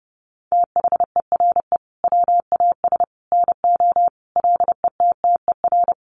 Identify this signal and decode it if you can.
Morse code, heard as a keyed tone, about 30 wpm, 700 Hz. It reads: THERE WAS NO LETTER
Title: Sentences from Top 300 Words 30wpm